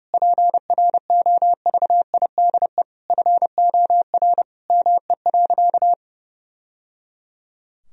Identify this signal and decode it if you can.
Morse code, heard as a keyed tone, about 32 wpm, 700 Hz. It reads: PROVIDE FOR ME.
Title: Sentences from Top 1000 Words - Review Part 3 - 30wpm